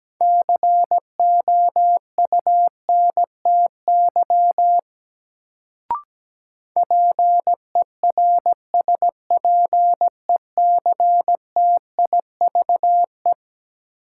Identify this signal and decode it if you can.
Morse code, heard as a keyed tone, about 17 wpm, 700 Hz. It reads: COUNTY PERSPECTIVE
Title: Sentences from Top 1200 Words - New Word Drill - 17wpm